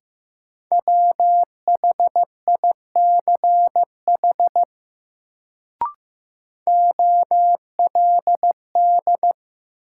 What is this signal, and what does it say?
Morse code, heard as a keyed tone, about 15 wpm, 700 Hz. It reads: WHICH OLD
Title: Sets of 2 Words - New Word Drill - 15wpm